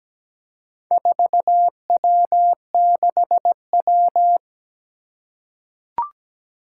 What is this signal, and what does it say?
Morse code, heard as a keyed tone, about 17 wpm, 700 Hz. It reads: 4W6W
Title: Four Challenging Letter-Number Combinations 17wpm